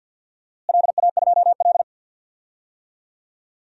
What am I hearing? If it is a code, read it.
Morse code, heard as a keyed tone, about 52 wpm, 700 Hz. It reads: LA2L
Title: DXCC-100 Call Signs - 4 Characters - Repeated 2x - 50wpm